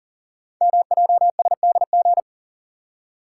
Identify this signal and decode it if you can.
Morse code, heard as a keyed tone, about 39 wpm, 700 Hz. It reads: MJSDG
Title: Code Groups - 5 Letters - 40wpm